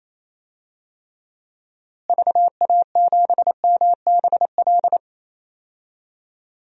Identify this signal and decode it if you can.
Morse code, heard as a keyed tone, about 28 wpm, 700 Hz. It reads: VA7MBL
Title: DXCC-100 Call Signs - 6 Characters - Repeated 2x - 28wpm